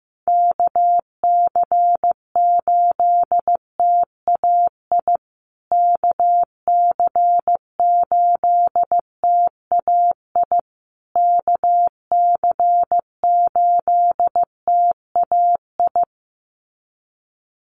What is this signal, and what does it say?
Morse code, heard as a keyed tone, about 15 wpm, 700 Hz. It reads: KC8TAI KC8TAI KC8TAI
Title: USA Call Signs Repeated 3x 15wpm